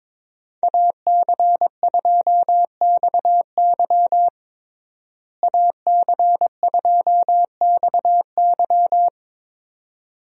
Morse code, heard as a keyed tone, about 22 wpm, 700 Hz. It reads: AC2XY AC2XY
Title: USA Call Signs - 5 Characters - Repeated 2x - 22wpm